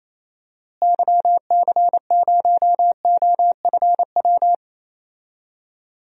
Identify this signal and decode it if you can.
Morse code, heard as a keyed tone, about 28 wpm, 700 Hz. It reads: YC0OFW
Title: DXCC-100 Call Signs - 6 Characters - Repeated 3x - 28wpm